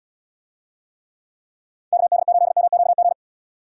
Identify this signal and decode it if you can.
Morse code, heard as a keyed tone, about 78 wpm, 700 Hz. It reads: KD8MQG